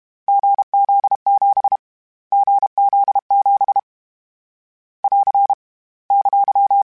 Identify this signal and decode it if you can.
Morse code, heard as a keyed tone, about 32 wpm, 800 Hz. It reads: GZ7 GZ7 <AR> !
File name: Difficult Combos / Difficult Combos-mp3s / awj1 ndb6 gz7 @30